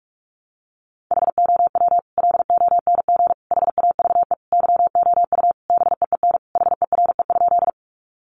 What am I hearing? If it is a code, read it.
Morse code, heard as a keyed tone, about 44 wpm, 700 Hz. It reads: HOW LONG HAVE YOU BEEN HERE?
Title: Sentences from Top 200 Words 45wpm